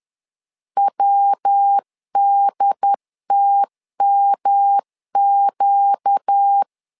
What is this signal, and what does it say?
Morse code, heard as a keyed tone, about 10 wpm, 800 Hz. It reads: WDTMQ
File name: LEZIONI / 4 Q1S 10wpm 2133